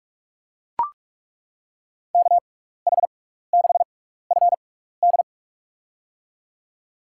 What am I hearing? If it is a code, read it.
Morse code, heard as a keyed tone, about 42 wpm, spaced as for 15 wpm, 700 Hz. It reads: KH6FD
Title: USA Call Signs - 5 Characters - 6x Character Spacing - 45wpm